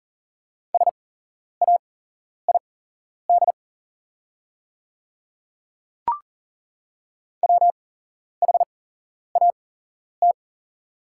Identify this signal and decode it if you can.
Morse code, heard as a keyed tone, about 40 wpm, spaced as for 11 wpm, 690 Hz. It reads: SAID WHAT
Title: 8x Character Spacing - Top 300 Words - 40wpm